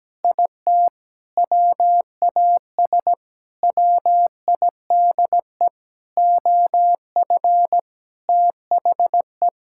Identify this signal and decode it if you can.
Morse code, heard as a keyed tone, about 17 wpm, 700 Hz. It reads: IT WAS WIDE OF THE
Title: Sentences from Top 1000 Words - Review Part 6 - 17wpm